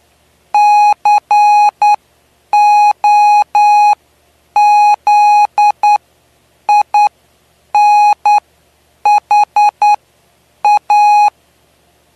Morse code, heard as a keyed tone, about 9 wpm, spaced as for 7 wpm, 830 Hz. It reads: COZINHA
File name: cw / py2lcd / 09-PY2LCD-Li��o 9